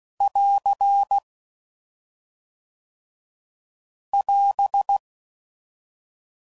Morse code, heard as a keyed tone, about 16 wpm, 800 Hz. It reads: <AR> <AS>